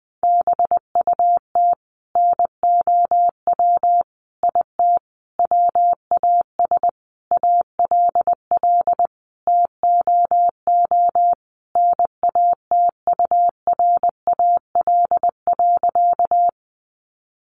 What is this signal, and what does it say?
Morse code, heard as a keyed tone, about 20 wpm, 700 Hz. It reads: BUT NOW IT WAS ALL TOO NATURAL.
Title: Sentences from Top 700 Words 20wpm